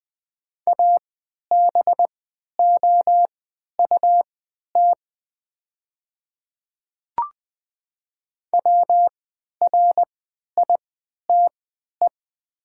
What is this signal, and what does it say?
Morse code, heard as a keyed tone, about 20 wpm, spaced as for 11 wpm, 700 Hz. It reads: ABOUT WRITE
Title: 3x Character Spacing - Top 100 Words - 20wpm